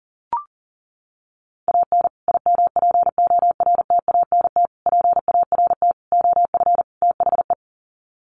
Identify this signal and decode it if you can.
Morse code, heard as a keyed tone, about 37 wpm, 700 Hz. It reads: AN IMPORTANT PART OF THE
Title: Sets of 5 Words Encore 40wpm